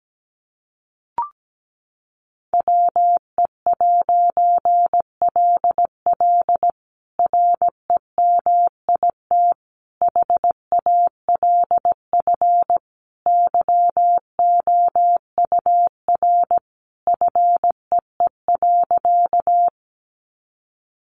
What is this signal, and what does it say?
Morse code, heard as a keyed tone, about 17 wpm, 700 Hz. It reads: WE'LL REMIT HALF YOUR FEE.
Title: A Sound Of Thunder 17wpm